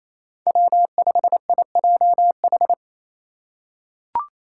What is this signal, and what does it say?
Morse code, heard as a keyed tone, about 27 wpm, 700 Hz. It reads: W5IJH
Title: K1USN SST QSO Practice - 28wpm